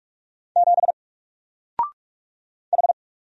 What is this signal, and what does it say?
Morse code, heard as a keyed tone, about 44 wpm, 700 Hz. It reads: 7 H
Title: Single Letter-Number Alt 45wpm